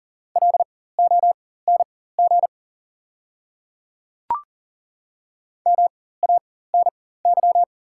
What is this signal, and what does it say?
Morse code, heard as a keyed tone, about 40 wpm, spaced as for 19 wpm, 690 Hz. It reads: LONG MANY
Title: 4x Character Spacing - Top 100 Words - 40wpm